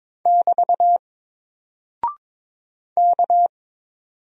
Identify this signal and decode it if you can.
Morse code, heard as a keyed tone, about 22 wpm, 700 Hz. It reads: <BT> K
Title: Single Character (Letter:Number:Punctuation:Prosign) 22wpm